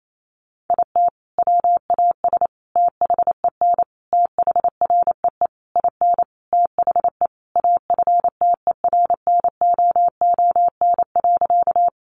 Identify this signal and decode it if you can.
Morse code, heard as a keyed tone, about 28 wpm, 700 Hz. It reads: IT WAS THEN THREE IN THE AFTERNOON.
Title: Sentences from Top 1500 Words 28wpm